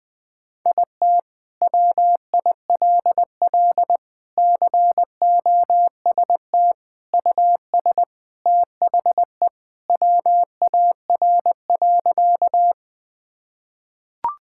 Morse code, heard as a keyed tone, about 21 wpm, 700 Hz. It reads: IT WILL COST US THE WAR.
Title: Sentences from Top 1000 Words - Review Part 5 - 20wpm